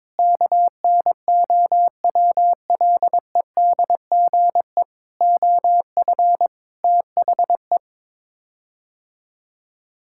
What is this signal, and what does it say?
Morse code, heard as a keyed tone, about 22 wpm, 700 Hz. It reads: KNOWLEDGE OF THE
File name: practice / 22wpm / Sets of 8 Words - 22wpm